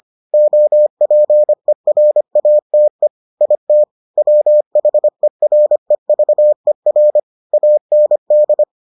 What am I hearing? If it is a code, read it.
Morse code, heard as a keyed tone, about 25 wpm, 600 Hz. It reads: OPERATE IT WHEREVER AND